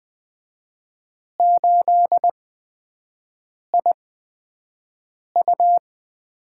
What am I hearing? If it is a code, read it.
Morse code, heard as a keyed tone, about 20 wpm, spaced as for 5 wpm, 700 Hz. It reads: 8IU
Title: USA Call Signs - 5 Characters - 8x Character Spacing - 20wpm